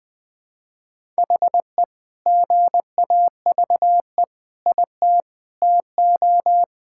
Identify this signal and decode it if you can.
Morse code, heard as a keyed tone, about 19 wpm, 700 Hz. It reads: HE GAVE IT TO
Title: Sentences with 2-Word Phrases - Spoken Beginning - 20wpm